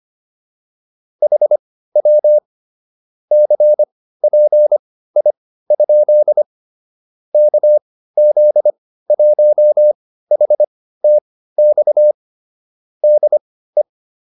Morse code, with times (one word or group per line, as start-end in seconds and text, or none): 1.2-2.4 HW
3.3-6.4 CPI?
7.3-12.1 KZ1HTX
13.0-13.8 DE